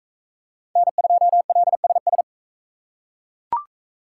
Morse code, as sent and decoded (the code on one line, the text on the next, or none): -. ..--- .-.. ... ...
N2LSS